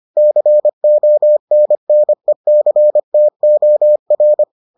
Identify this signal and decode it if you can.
Morse code, heard as a keyed tone, about 25 wpm, 600 Hz. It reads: CONNECTOR